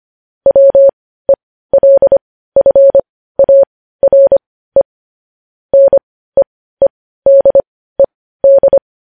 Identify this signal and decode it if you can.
Morse code, heard as a keyed tone, about 25 wpm, spaced as for 15 wpm, 550 Hz. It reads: WELFARE NEEDED
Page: news-headlines-in-morse-code-at-15-wpm